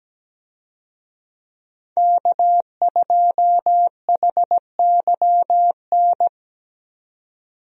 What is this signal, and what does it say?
Morse code, heard as a keyed tone, about 17 wpm, 700 Hz. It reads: K2HYN
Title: USA Call Signs - 5 Characters - 17wpm